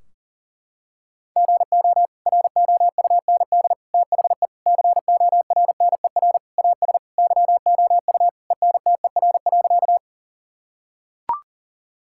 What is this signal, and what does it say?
Morse code, heard as a keyed tone, about 40 wpm, 700 Hz. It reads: GO ROUND THE CORNER AS YOU ENTER.
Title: Sentences from Top 1600 Words 40wpm